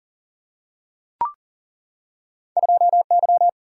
Morse code, heard as a keyed tone, about 40 wpm, 700 Hz. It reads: 2Y
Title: Single-Letter Single-Number 40wpm